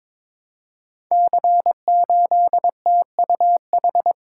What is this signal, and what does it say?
Morse code, heard as a keyed tone, about 22 wpm, 700 Hz. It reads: C8TUH
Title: Code Groups - 5 Letter:Numbers - 22wpm